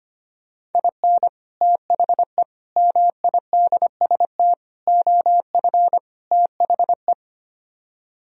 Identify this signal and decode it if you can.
Morse code, heard as a keyed tone, about 33 wpm, 700 Hz. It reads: IN THE MIDST OF THE